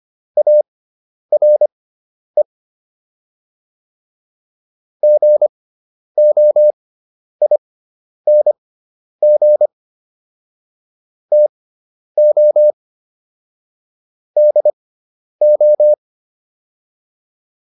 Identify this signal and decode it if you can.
Morse code, heard as a keyed tone, about 25 wpm, spaced as for 10 wpm, 600 Hz. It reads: ARE GOING TO DO